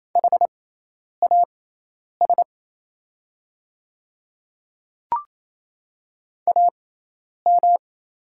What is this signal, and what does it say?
Morse code, heard as a keyed tone, about 28 wpm, spaced as for 10 wpm, 690 Hz. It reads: HAS AM
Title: 6x Character Spacing - Top 500 Words - 28wpm